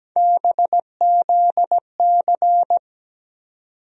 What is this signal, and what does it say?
Morse code, heard as a keyed tone, about 17 wpm, 700 Hz. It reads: BZC
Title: USA Call Sign Postfixes - 3 Letters 17wpm